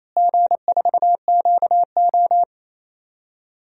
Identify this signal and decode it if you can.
Morse code, heard as a keyed tone, about 28 wpm, 700 Hz. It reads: G4QO
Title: DXCC-100 Call Signs - 4 Characters - Repeated 3x - 28wpm